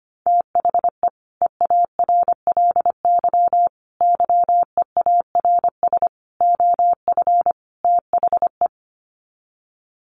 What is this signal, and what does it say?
Morse code, heard as a keyed tone, about 25 wpm, 700 Hz. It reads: THE EARLY YEARS OF THE
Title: Sets of 5 Words Encore 25wpm